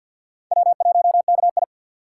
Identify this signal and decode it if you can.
Morse code, heard as a keyed tone, about 60 wpm, 690 Hz. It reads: W1KI
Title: Speed Racing - USA 1x2 Call Signs 50wpm